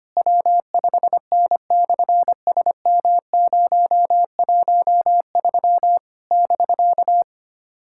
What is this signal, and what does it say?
Morse code, heard as a keyed tone, about 24 wpm, 700 Hz. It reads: W5N/SM013 <BK>
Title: SOTA QSO Practice 25wpm